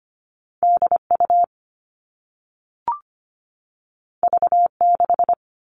Morse code, heard as a keyed tone, about 25 wpm, 700 Hz. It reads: DU V6
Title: All Cut Numbers - 2-Digit 25wpm